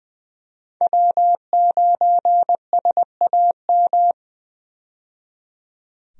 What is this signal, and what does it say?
Morse code, heard as a keyed tone, about 20 wpm, 700 Hz. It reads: W9SAM
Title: USA Call Signs - 5 Characters - 20wpm